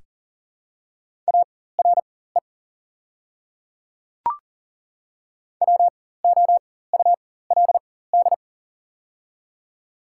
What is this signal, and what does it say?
Morse code, heard as a keyed tone, about 40 wpm, spaced as for 19 wpm, 700 Hz. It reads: ARE WOULD
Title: 4x Character Spacing - Top 100 Words - 40wpm